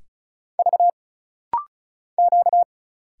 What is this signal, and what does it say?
Morse code, heard as a keyed tone, about 34 wpm, 700 Hz. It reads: V Q
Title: Single Letters - Rapid-Fire 35wpm